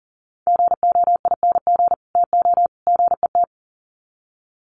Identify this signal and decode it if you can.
Morse code, heard as a keyed tone, about 38 wpm, 700 Hz. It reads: GOING TO GET
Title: Sets of 3 Words 40wpm